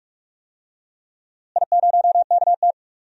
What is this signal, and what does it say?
Morse code, heard as a keyed tone, about 46 wpm, 700 Hz. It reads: I0KT
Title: DXCC-100 Call Signs - 4 Characters - Repeated 3x - 45wpm